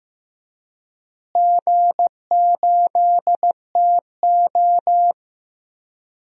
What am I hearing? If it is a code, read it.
Morse code, heard as a keyed tone, about 15 wpm, 700 Hz. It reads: G8TO
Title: DXCC-100 Call Signs - 4 Characters - Repeated 3x - 15wpm